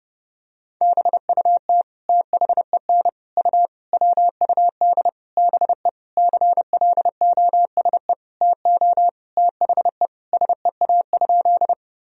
Morse code, heard as a keyed tone, about 32 wpm, 700 Hz. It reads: BUT THEN U WUD BE CLOSE TO THE SEA?
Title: Sentences from Top 300 Words and Abbreviations 30wpm